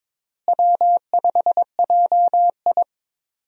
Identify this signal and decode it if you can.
Morse code, heard as a keyed tone, about 22 wpm, 700 Hz. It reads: W5JI
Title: USA Call Signs - 4 Characters - Repeated 3x - 22wpm